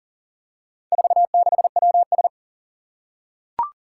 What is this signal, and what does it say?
Morse code, heard as a keyed tone, about 40 wpm, 700 Hz. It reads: 46WS
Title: Four Challenging Letter-Number Combinations 40wpm